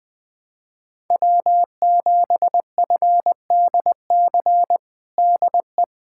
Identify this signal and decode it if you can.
Morse code, heard as a keyed tone, about 21 wpm, 700 Hz. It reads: W7FDC DE
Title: USA Callsign DE Callsign - 5 Characters - 20wpm